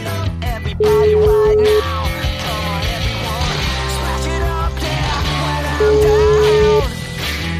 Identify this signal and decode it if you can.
Ringback tone (Europe, 425 Hz), with noise.